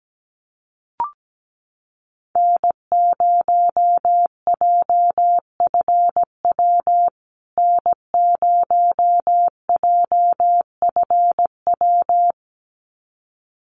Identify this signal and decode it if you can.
Morse code, heard as a keyed tone, about 17 wpm, 700 Hz. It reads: N0JFW N0JFW
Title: USA Call Signs Repeated 2x 17wpm